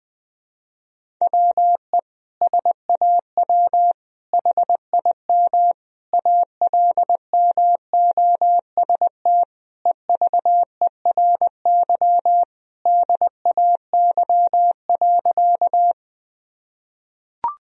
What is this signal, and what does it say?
Morse code, heard as a keyed tone, about 20 wpm, 700 Hz. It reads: WE SAW HIM ALMOST EVERY DAY.
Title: Sentences from Top 1000 Words - Review Part 4 - 20wpm